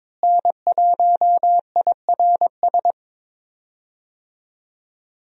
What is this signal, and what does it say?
Morse code, heard as a keyed tone, about 22 wpm, 700 Hz. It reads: N1IRS
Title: USA Call Signs - 5 Characters - Repeated 3x - 22wpm